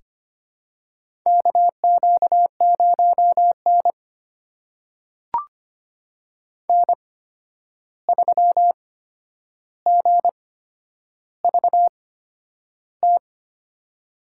Morse code, heard as a keyed tone, about 25 wpm, spaced as for 10 wpm, 700 Hz. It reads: KQ0N N3GVT
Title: USA Call Signs - 8x Character Spacing - 25wpm